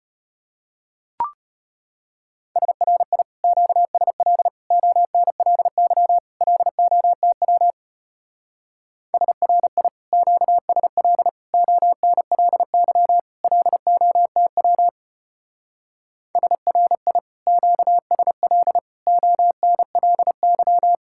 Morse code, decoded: SRI QSL ONLY LOTW SRI QSL ONLY LOTW SRI QSL ONLY

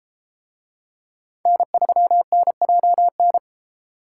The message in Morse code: N3NJN

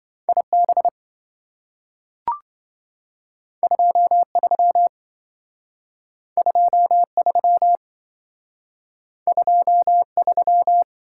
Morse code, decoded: IB 23 23 23

